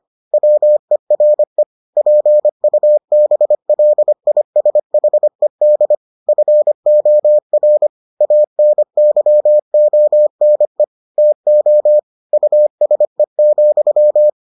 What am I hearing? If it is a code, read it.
Morse code, heard as a keyed tone, about 25 wpm, 600 Hz. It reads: WERE PUBLISHED FOR ANYONE TO USE,